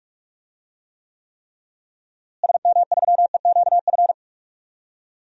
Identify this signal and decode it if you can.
Morse code, heard as a keyed tone, about 41 wpm, 700 Hz. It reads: SM3EQF
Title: DXCC-100 Call Signs - 6 Characters - Repeated 3x - 45wpm